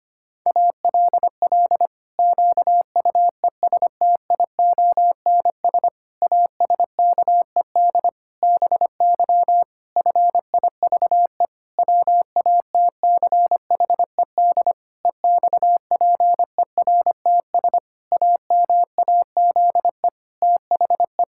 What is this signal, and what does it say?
Morse code, heard as a keyed tone, about 25 wpm, 700 Hz. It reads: ALL QUESTIONS ASKED BY FIVE WATCHED EXPERTS AMAZE THE
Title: Pangram Challenge v2 - 25wpm